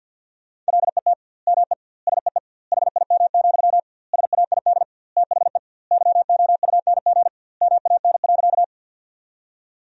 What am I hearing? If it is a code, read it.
Morse code, heard as a keyed tone, about 50 wpm, 700 Hz. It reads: LET ME SEE HIM, SAID THE YOUNG MAN.
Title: 2x Word-Spacing - Sentences from Top 400 Words - 50wpm